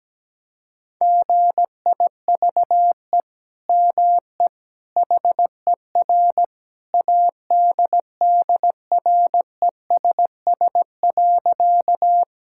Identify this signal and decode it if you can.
Morse code, heard as a keyed tone, about 17 wpm, 700 Hz. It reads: GIVE ME HER ADDRESS.
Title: Sentences from Top 1000 Words - Review Part 6 - 17wpm